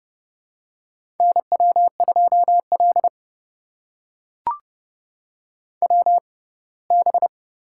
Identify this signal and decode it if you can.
Morse code, heard as a keyed tone, about 30 wpm, spaced as for 20 wpm, 700 Hz. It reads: NW2L WB
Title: USA Call Signs - 6x Character Spacing - 30wpm